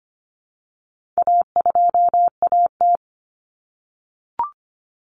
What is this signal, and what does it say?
Morse code, heard as a keyed tone, about 25 wpm, 700 Hz. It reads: A2AT
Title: Common Cut Numbers - 4-Digit 25wpm